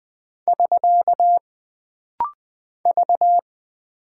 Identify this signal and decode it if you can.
Morse code, heard as a keyed tone, about 20 wpm, 700 Hz. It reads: <SK> V
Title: Single Character (Letter:Number:Punctuation:Prosign) - Rapid-Fire - 20wpm